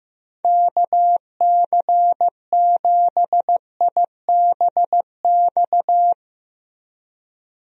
Morse code, heard as a keyed tone, about 15 wpm, 700 Hz. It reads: KC7IBX
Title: USA Call Signs - 6 Characters - Repeated 3x - 15wpm